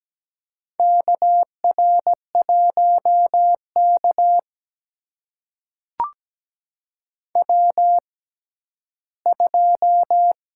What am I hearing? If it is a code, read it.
Morse code, heard as a keyed tone, about 17 wpm, spaced as for 11 wpm, 700 Hz. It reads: KR1K W2
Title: USA Call Signs - 4 Characters - 6x Character Spacing - 17wpm